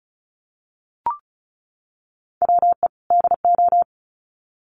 Morse code, heard as a keyed tone, about 34 wpm, 700 Hz. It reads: WE DO
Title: Sets of 2 Words 35wpm